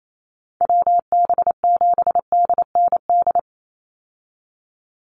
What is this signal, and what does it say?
Morse code, heard as a keyed tone, about 28 wpm, 700 Hz. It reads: WB7DND
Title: Call Signs Repeated 2x - 28wpm